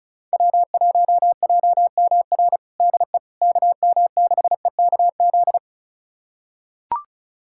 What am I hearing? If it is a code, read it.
Morse code, heard as a keyed tone, about 34 wpm, 700 Hz. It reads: W1JMR DE KM6EKZ
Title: USA Callsign DE Callsign 35wpm